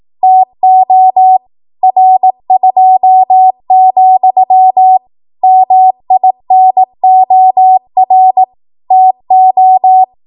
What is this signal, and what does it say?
Morse code, heard as a keyed tone, about 18 wpm, 750 Hz. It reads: TO R2, MINOR TO
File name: Morse / 18WPM